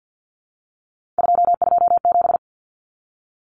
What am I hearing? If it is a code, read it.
Morse code, heard as a keyed tone, about 50 wpm, 700 Hz. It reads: 227
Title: Common Cut Numbers - 3-Digit 50wpm